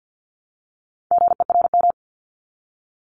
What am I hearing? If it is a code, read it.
Morse code, heard as a keyed tone, about 50 wpm, 700 Hz. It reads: GERM
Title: Multi-letter Phonemes Repeated 3x 50wpm